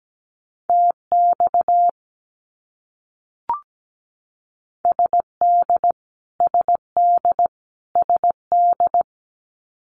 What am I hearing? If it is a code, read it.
Morse code, heard as a keyed tone, about 17 wpm, 700 Hz. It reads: TX SD SD SD
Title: US States Repeated 3x 17wpm